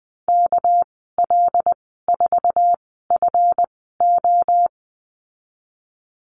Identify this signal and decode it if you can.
Morse code, heard as a keyed tone, about 20 wpm, spaced as for 14 wpm, 700 Hz. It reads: KL4FO
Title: 207 Lesson 33g - Letter j - Call Signs 20wpm